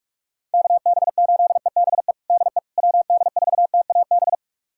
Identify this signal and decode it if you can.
Morse code, heard as a keyed tone, about 46 wpm, 700 Hz. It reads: KB8EBE DE WD4TAB
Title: USA Callsign DE Callsign - 6 Characters - 45wpm